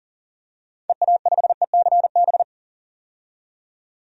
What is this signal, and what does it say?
Morse code, heard as a keyed tone, about 40 wpm, 700 Hz. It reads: EA5ECB
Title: DXCC-100 Call Signs - 6 Characters - 40wpm